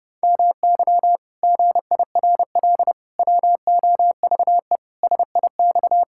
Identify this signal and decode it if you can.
Morse code, heard as a keyed tone, about 32 wpm, 700 Hz. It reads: MY GIRL WOVE SIX
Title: Pangram Challenge v2 - 30wpm